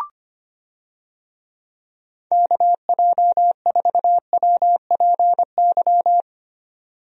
Morse code, heard as a keyed tone, about 25 wpm, 700 Hz. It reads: KJ4WPY
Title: USA Call Signs - 6 Characters - 25wpm